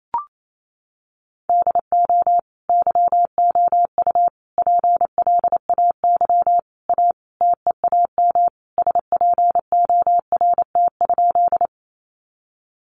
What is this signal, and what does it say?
Morse code, heard as a keyed tone, about 28 wpm, 700 Hz. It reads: DO YOU PLAY A TEAM SPORT?
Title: Sentences from Top 1500 Words 28wpm